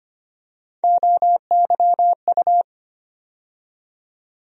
Morse code, heard as a keyed tone, about 25 wpm, 700 Hz. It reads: OYU
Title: USA Call Sign Postfixes - 3 Letters 25wpm